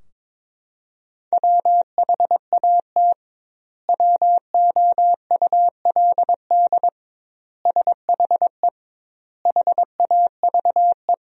Morse code, heard as a keyed tone, about 22 wpm, 700 Hz. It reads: WHAT WOULD SHE HAVE
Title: 2x Word-Spacing - Sentences from Top 100 Words - 22wpm